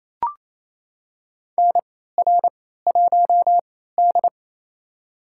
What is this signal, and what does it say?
Morse code, heard as a keyed tone, about 28 wpm, spaced as for 16 wpm, 700 Hz. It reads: NR1D
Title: USA Call Signs - 4 Characters - 3x Character Spacing - 28wpm